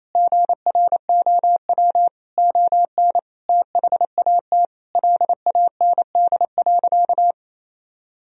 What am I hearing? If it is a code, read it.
Morse code, heard as a keyed tone, about 27 wpm, 700 Hz. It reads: GROW ON THAT LAND.
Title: Sentences from Top 1000 Words - Review Part 1 - 28wpm